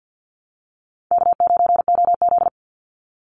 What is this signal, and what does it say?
Morse code, heard as a keyed tone, about 48 wpm, 700 Hz. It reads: K9OZ
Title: Easy Call Signs Repeated 3x - 50wpm